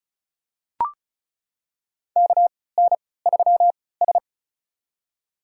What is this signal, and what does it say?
Morse code, heard as a keyed tone, about 34 wpm, spaced as for 20 wpm, 690 Hz. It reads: KN3S
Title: USA Call Signs - 3x Character Spacing - 35wpm